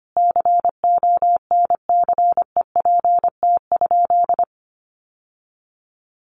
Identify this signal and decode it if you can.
Morse code, heard as a keyed tone, about 25 wpm, 700 Hz. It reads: CONCEPT?
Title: Sentences from Top 1100 Words 25wpm